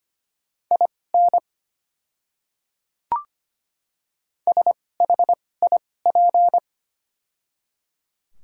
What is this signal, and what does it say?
Morse code, heard as a keyed tone, about 25 wpm, spaced as for 18 wpm, 700 Hz. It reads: IN SHIP
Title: 2x Character Spacing - Top 500 Words - 25wpm